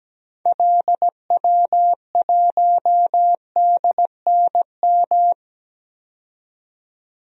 Morse code, heard as a keyed tone, about 17 wpm, 700 Hz. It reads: LW1DNM
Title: DXCC-100 Call Signs - 6 Characters - 17wpm